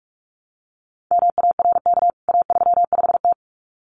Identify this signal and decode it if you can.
Morse code, heard as a keyed tone, about 44 wpm, 700 Hz. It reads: MARK A35T
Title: CWT Exchange 45wpm